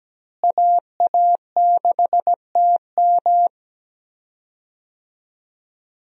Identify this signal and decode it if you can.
Morse code, heard as a keyed tone, about 17 wpm, 700 Hz. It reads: AA6TM
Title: USA Call Signs - 5 Characters - Repeated 3x - 17wpm